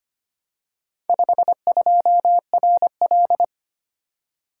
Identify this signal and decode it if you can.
Morse code, heard as a keyed tone, about 25 wpm, 700 Hz. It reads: 52RL
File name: practice / 25wpm / Four Letter-Number Combinations - 25wpm